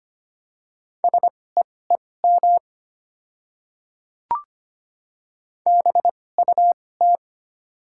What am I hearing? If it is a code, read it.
Morse code, heard as a keyed tone, about 25 wpm, spaced as for 18 wpm, 700 Hz. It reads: SEEM BUT
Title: 2x Character Spacing - Top 300 Words - 25wpm